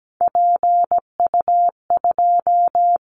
Morse code, heard as a keyed tone, about 17 wpm, 700 Hz. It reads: PU2
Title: Callsign Prefixes for Top 50 DXCC - 17wpm